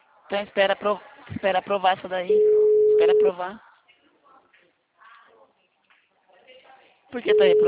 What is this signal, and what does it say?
Ringback tone (Europe, 425 Hz), with speech.